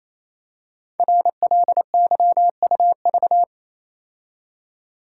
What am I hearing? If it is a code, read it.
Morse code, heard as a keyed tone, about 28 wpm, 700 Hz. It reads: RLYUV
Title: Code Groups - 5 Letters - 28wpm